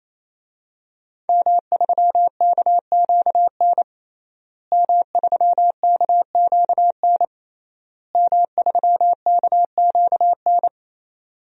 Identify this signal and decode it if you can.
Morse code, heard as a keyed tone, about 27 wpm, 700 Hz. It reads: M3KQN M3KQN M3KQN
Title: DXCC-100 Call Signs - 5 Characters - Repeated 3x - 28wpm